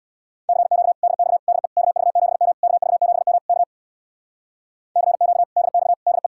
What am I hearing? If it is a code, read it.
Morse code, heard as a keyed tone, about 64 wpm, 700 Hz. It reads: CQ DX DE KV7M KV7M K CQ DX DE